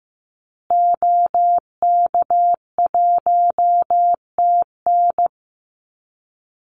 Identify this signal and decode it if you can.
Morse code, heard as a keyed tone, about 15 wpm, 700 Hz. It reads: OK1TN
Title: Call Signs Repeated 3x - 15wpm